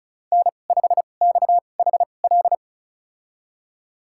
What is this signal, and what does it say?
Morse code, heard as a keyed tone, about 34 wpm, spaced as for 25 wpm, 700 Hz. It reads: N5XHL
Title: USA Call Signs - 2x Character Spacing - 35wpm